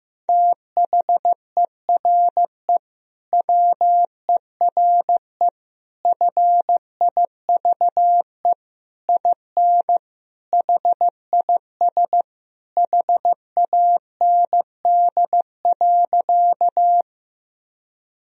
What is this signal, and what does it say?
Morse code, heard as a keyed tone, about 15 wpm, 700 Hz. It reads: THERE WERE FIVE IN HIS HAND.
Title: Sentences from Top 1000 Words - Review Part 2 - 15wpm